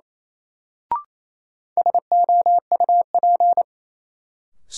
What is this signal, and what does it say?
Morse code, heard as a keyed tone, about 28 wpm, 700 Hz. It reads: SOUP
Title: Sentences from Top 1000 Words - Review Part 6 - New Word Drill - 28wpm